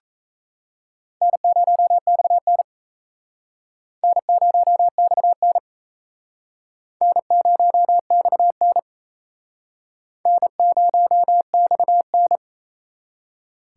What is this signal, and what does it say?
Morse code, heard as a keyed tone, about 38 wpm, 700 Hz. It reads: N0XN N0XN N0XN N0XN